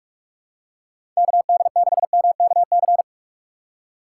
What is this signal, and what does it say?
Morse code, heard as a keyed tone, about 44 wpm, 700 Hz. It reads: KD6MKC
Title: USA Call Signs - 6 Characters - 4x Character Spacing - 45wpm